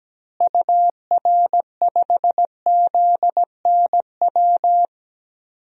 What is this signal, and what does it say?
Morse code, heard as a keyed tone, about 17 wpm, 700 Hz. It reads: UR5ZNW
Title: DXCC-100 Call Signs - 6 Characters - Repeated 3x - 17wpm